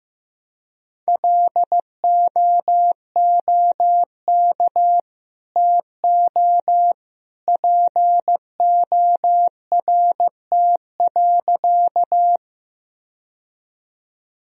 Morse code, heard as a keyed tone, about 15 wpm, 700 Hz. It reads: LOOK TO PORT.
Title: Sentences from Top 1000 Words - Review Part 1 - 15wpm